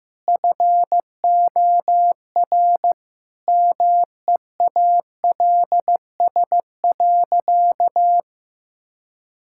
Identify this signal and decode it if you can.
Morse code, heard as a keyed tone, about 16 wpm, 700 Hz. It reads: FOR MEALS.
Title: Pangram Challenge v2 - 15wpm